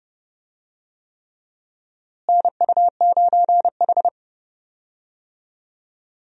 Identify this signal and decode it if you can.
Morse code, heard as a keyed tone, about 32 wpm, 700 Hz. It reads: NU9H
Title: USA Call Signs - 4 Characters - Repeated 3x - 30wpm